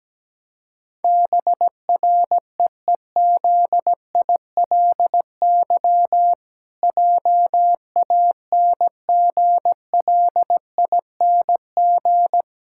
Morse code, heard as a keyed tone, about 17 wpm, 700 Hz. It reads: BREEZILY JANGLING
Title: Pangram Challenge v2 - 17wpm